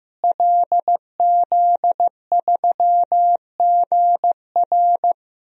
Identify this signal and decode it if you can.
Morse code, heard as a keyed tone, about 15 wpm, 700 Hz. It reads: LZ3GR
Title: DXCC-100 Call Signs - 5 Characters - Repeated 3x - 15wpm